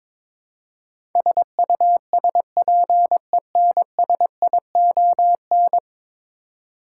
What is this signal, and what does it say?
Morse code, heard as a keyed tone, about 22 wpm, 700 Hz. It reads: SUSPENSION
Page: morse-code-ninja-practice-22wpm